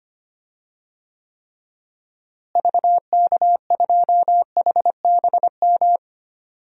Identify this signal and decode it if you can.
Morse code, heard as a keyed tone, about 25 wpm, 700 Hz. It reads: VK2HBM
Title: DXCC-100 Call Signs - 6 Characters - Repeated 3x - 25wpm